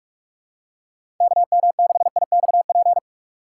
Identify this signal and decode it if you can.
Morse code, heard as a keyed tone, about 43 wpm, 700 Hz. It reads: KM6IXP
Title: USA Call Signs - 6 Characters - 2x Character Spacing - 45wpm